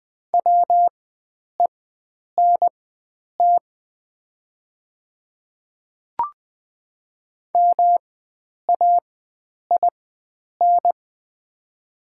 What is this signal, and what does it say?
Morse code, heard as a keyed tone, about 20 wpm, spaced as for 9 wpm, 700 Hz. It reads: WENT MAIN
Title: 4x Character Spacing - Top 500 Words - 20wpm